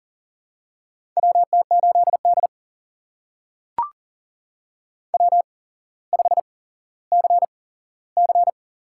Morse code, heard as a keyed tone, about 40 wpm, spaced as for 17 wpm, 700 Hz. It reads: WT8D W5CC
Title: USA Call Signs - 4 Characters - 8x Character Spacing - 40wpm